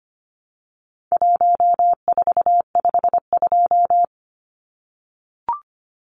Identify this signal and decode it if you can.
Morse code, heard as a keyed tone, about 25 wpm, 700 Hz. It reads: 1452
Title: Numbers - Contest Style - 4-Digit 25wpm